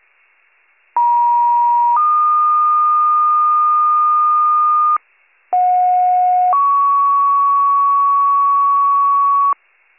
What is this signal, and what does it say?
Two-tone page, heard as a one-second tone, then a three-second tone, about 950 then 1220 Hz, about 730 then 1120 Hz. It is heard from 1.0 to 9.5 s.